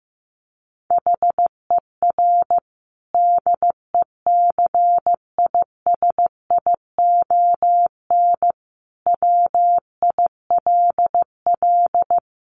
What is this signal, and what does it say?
Morse code, heard as a keyed tone, about 15 wpm, 700 Hz. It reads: HER DECISION WILL
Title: Sentences from Top 1300 Words 15wpm